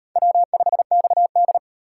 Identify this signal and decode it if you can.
Morse code, heard as a keyed tone, about 36 wpm, 700 Hz. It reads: W5XD